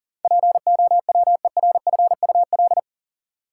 Morse code, heard as a keyed tone, about 40 wpm, 700 Hz. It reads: POWERFUL
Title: Sentences from Top 1300 Words - New Word Drill Alt - 40wpm